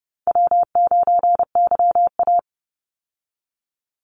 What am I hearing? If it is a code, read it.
Morse code, heard as a keyed tone, about 30 wpm, 700 Hz. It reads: W9YA
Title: Call Signs Repeated 2x - 30wpm